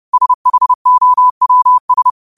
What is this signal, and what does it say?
Morse code, heard as a keyed tone, about 30 wpm, 1000 Hz. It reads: SHOWS
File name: qotdfiles / 30wpm